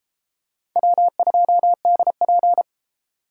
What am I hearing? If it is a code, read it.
Morse code, heard as a keyed tone, about 32 wpm, 700 Hz. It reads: W2DP